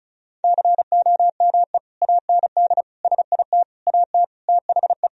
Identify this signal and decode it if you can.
Morse code, heard as a keyed tone, about 36 wpm, 700 Hz. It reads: COME AND SIT AT THE
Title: Sentences from Top 1000 Words - Review Part 2 - 35wpm